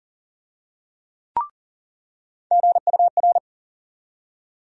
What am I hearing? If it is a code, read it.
Morse code, heard as a keyed tone, about 40 wpm, 700 Hz. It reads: GUR
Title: USA Call Sign Postfixes - 3 Letters 40wpm